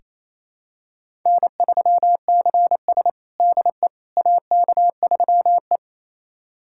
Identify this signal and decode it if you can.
Morse code, heard as a keyed tone, about 28 wpm, 700 Hz. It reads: N3CS DE AK3E
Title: USA Callsign DE Callsign - 4 Characters - 28wpm